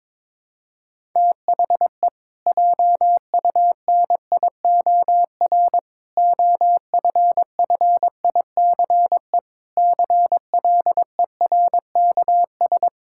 Morse code, heard as a keyed tone, about 22 wpm, 700 Hz. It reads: THE JUNIOR OFFICE CLERKS
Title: Pangram Challenge v2 - 22wpm